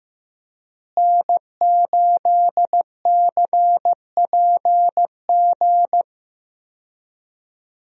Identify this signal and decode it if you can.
Morse code, heard as a keyed tone, about 15 wpm, 700 Hz. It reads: N8CPG
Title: USA Call Signs - 5 Characters - Repeated 3x - 15wpm